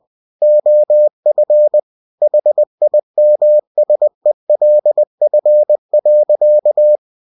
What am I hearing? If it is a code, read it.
Morse code, heard as a keyed tone, about 20 wpm, 600 Hz. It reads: OF HIMSELF.
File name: winnie-20wpm